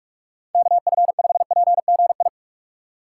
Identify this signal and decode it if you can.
Morse code, heard as a keyed tone, about 46 wpm, 700 Hz. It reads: KF5PGI
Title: USA Call Signs - 2x Character Spacing - 45wpm